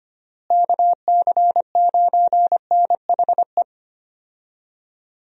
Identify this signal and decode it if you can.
Morse code, heard as a keyed tone, about 25 wpm, 700 Hz. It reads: KC9NHE
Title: USA Call Signs - 6 Characters - Repeated 3x - 25wpm